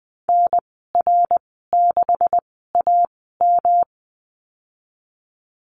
Morse code, heard as a keyed tone, about 20 wpm, spaced as for 15 wpm, 700 Hz. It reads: NR6AM
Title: 161 Lesson 26g - Letter g - Call Signs 20wpm